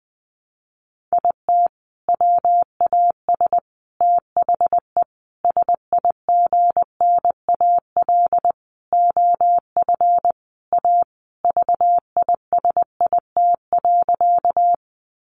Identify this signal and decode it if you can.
Morse code, heard as a keyed tone, about 20 wpm, 700 Hz. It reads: IT WAS THE SIGNAL OF A VISIT.
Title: Sentences from Top 1400 Words 20wpm